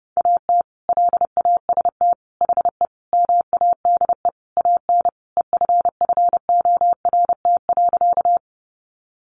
Morse code, heard as a keyed tone, about 30 wpm, 700 Hz. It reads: AT LAST HE MADE AN EFFORT.
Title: Sentences from Top 1200 Words 30wpm